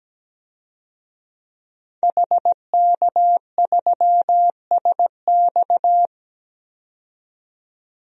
Morse code, heard as a keyed tone, about 17 wpm, 700 Hz. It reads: HK3SX
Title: DXCC-100 Call Signs - 5 Characters - 17wpm